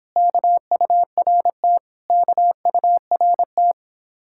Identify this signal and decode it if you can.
Morse code, heard as a keyed tone, about 31 wpm, 700 Hz. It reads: KURT KURT